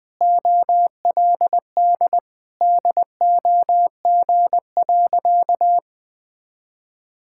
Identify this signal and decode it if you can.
Morse code, heard as a keyed tone, about 20 wpm, 700 Hz. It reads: OLD DOG.
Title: Sentences from Top 1000 Words - Review Part 6 - 20wpm